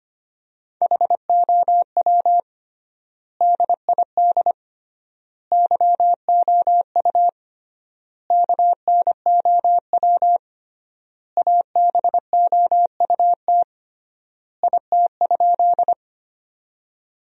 Morse code, heard as a keyed tone, about 25 wpm, 700 Hz. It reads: HOW DID YOU KNOW ABOUT IT?
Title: 3x Word-Spacing - Sentences from Top 100 Words - 25wpm